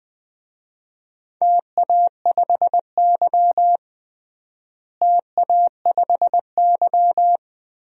Morse code, heard as a keyed tone, about 20 wpm, 700 Hz. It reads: TA5Y TA5Y
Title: DXCC-100 Call Signs - 4 Characters - Repeated 2x - 20wpm